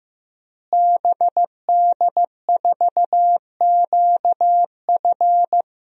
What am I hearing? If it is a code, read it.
Morse code, heard as a keyed tone, about 15 wpm, 700 Hz. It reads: BD4QF
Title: DXCC-100 Call Signs - 5 Characters - Repeated 3x - 15wpm